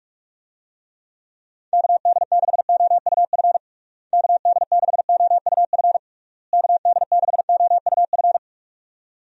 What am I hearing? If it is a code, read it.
Morse code, heard as a keyed tone, about 46 wpm, 700 Hz. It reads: KD6OUF KD6OUF KD6OUF
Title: USA Call Signs - 6 Characters - Repeated 3x - 45wpm